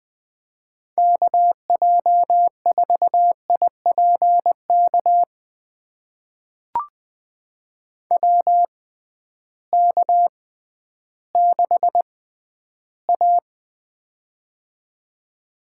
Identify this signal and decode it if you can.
Morse code, heard as a keyed tone, about 21 wpm, spaced as for 12 wpm, 700 Hz. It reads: KJ4IPK WK6A
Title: USA Call Signs - 6x Character Spacing - 20wpm